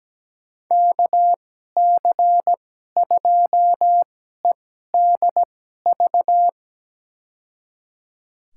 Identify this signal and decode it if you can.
Morse code, heard as a keyed tone, about 17 wpm, spaced as for 12 wpm, 700 Hz. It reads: KC2EDV
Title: USA Call Signs - 2x Character Spacing - 17wpm